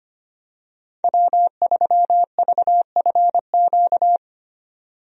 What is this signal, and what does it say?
Morse code, heard as a keyed tone, about 25 wpm, 700 Hz. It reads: W3VFQ
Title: USA Call Signs - 5 Characters - 25wpm